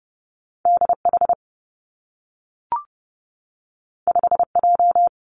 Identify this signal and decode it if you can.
Morse code, heard as a keyed tone, about 30 wpm, 700 Hz. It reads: D H 5 J
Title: Two Challenging Letter-Number Combinations 30wpm